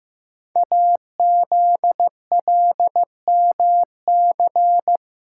Morse code, heard as a keyed tone, about 15 wpm, 700 Hz. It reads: AZLMC
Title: Code Groups - 5 Letters - 15wpm